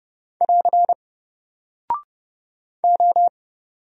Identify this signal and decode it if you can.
Morse code, heard as a keyed tone, about 30 wpm, 700 Hz. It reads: <AR> O
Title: Single Character (Letter:Number:Punctuation:Prosign) Alt 30wpm